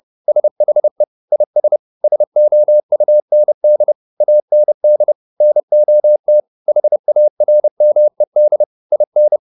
Morse code, heard as a keyed tone, about 30 wpm, 600 Hz. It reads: SHE IS SOUND AND NOT HARMED IN